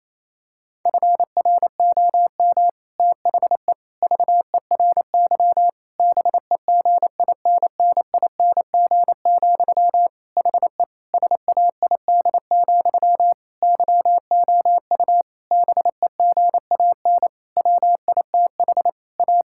Morse code, heard as a keyed tone, about 27 wpm, 700 Hz. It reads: FROM THE VERY BEGINNING, HE SAID, YOU BEGAN WITH A
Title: Sentences from Top 1600 Words 28wpm